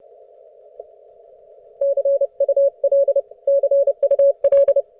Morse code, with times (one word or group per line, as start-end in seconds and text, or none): 1.8-3.2 CUL
3.5-4.8 CUL